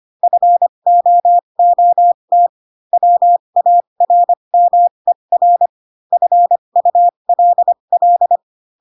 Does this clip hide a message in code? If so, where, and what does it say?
Yes: Morse code from 0.2 to 8.4 s, heard as a keyed tone, about 25 wpm, spaced as for 22 wpm, 700 Hz. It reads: FOOT WARMER FULL